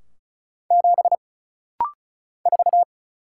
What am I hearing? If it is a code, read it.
Morse code, heard as a keyed tone, about 34 wpm, 700 Hz. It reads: 7 4
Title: Single Letter-Number Alt - Rapid-Fire 35wpm